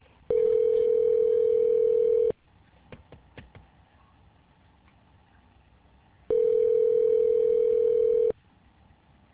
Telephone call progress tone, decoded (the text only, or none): ringback tone